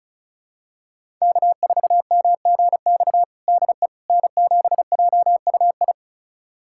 Morse code, heard as a keyed tone, about 36 wpm, 700 Hz. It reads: K4MGX DE N7JUI
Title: USA Callsign DE Callsign - 5 Characters - 35wpm